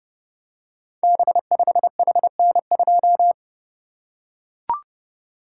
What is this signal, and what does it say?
Morse code, heard as a keyed tone, about 30 wpm, 700 Hz. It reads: B5HN2
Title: Code Groups - 5 Letter:Numbers - 30wpm